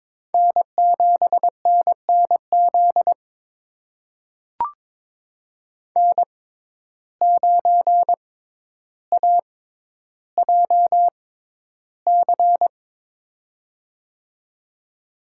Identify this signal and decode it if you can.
Morse code, heard as a keyed tone, about 22 wpm, spaced as for 11 wpm, 700 Hz. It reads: N7NNZ N9AJC
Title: USA Call Signs - 5 Characters - 6x Character Spacing - 22wpm